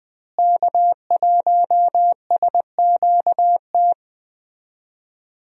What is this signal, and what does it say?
Morse code, heard as a keyed tone, about 20 wpm, 700 Hz. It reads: K1SQT
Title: USA Call Signs - 5 Characters - Repeated 3x - 20wpm